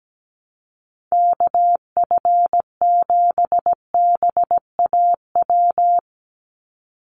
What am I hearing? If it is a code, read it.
Morse code, heard as a keyed tone, about 17 wpm, 700 Hz. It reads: KF7BAW
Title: USA Call Signs 17wpm